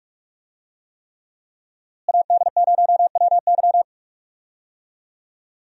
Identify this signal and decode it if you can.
Morse code, heard as a keyed tone, about 46 wpm, 700 Hz. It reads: AD0WY
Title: USA Call Signs - 5 Characters - Repeated 3x - 45wpm